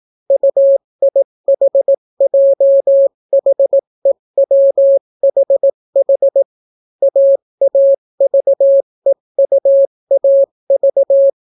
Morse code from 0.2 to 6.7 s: UIHJHEWHH